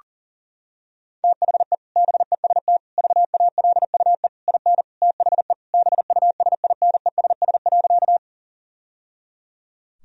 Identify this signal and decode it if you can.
Morse code, heard as a keyed tone, about 40 wpm, 700 Hz. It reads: THE BEST VALUE IN THE BUSINESS.
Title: Sentences from Top 1000 Words - Review Part 3 - 40wpm